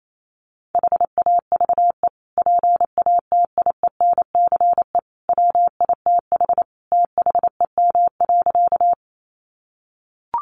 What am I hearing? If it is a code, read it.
Morse code, heard as a keyed tone, about 28 wpm, 700 Hz. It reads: HAVE PATIENCE WITH THEM.
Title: Sentences from Top 1300 Words 28wpm